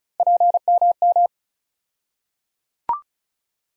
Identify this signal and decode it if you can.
Morse code, heard as a keyed tone, about 34 wpm, 700 Hz. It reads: PMM
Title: USA Call Sign Postfixes - 3 Letters 35wpm